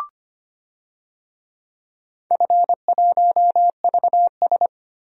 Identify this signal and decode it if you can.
Morse code, heard as a keyed tone, about 25 wpm, 700 Hz. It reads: F1VS
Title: DXCC-100 Call Signs - 4 Characters - Repeated 2x - 25wpm